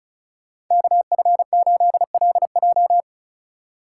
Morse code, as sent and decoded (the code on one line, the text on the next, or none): -.- ..-. ---.. .-.. .---
KF8LJ